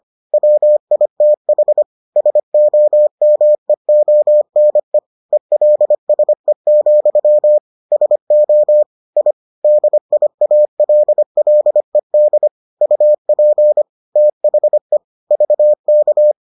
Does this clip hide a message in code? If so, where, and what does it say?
Yes: Morse code from 0.3 to 16.3 s, heard as a keyed tone, about 25 wpm, 600 Hz. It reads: WITH SOMEONE ELSE, SO I DIALLED UP THE VK